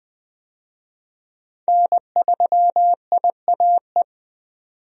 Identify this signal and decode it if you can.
Morse code, heard as a keyed tone, about 20 wpm, 700 Hz. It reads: N3IAE
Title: USA Call Signs - 5 Characters - 20wpm